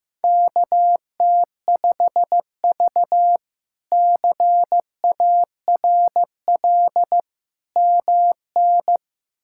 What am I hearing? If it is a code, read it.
Morse code, heard as a keyed tone, about 15 wpm, 700 Hz. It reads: KT5V CARL MN
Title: K1USN SST QSO Practice - 15wpm